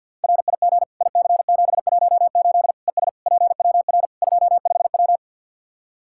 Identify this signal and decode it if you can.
Morse code, heard as a keyed tone, about 52 wpm, 700 Hz. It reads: RIG IC718 ES PWR 25W
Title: Practice QSO - Exchange 2 - Info - 50wpm